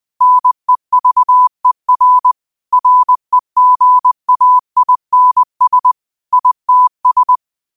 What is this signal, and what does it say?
Morse code, heard as a keyed tone, about 20 wpm, 1000 Hz. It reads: NEVER REGAINS ITS
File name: qotdfiles / 20wpm